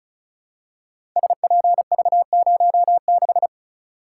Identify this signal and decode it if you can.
Morse code, heard as a keyed tone, about 36 wpm, 690 Hz. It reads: SPV06
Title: Code Groups - 5 Letter:Numbers - 35wpm